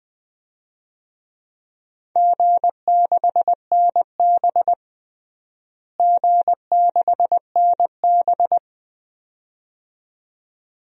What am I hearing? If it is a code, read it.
Morse code, heard as a keyed tone, about 20 wpm, 700 Hz. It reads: G6NB G6NB
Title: DXCC-100 Call Signs - 4 Characters - Repeated 2x - 20wpm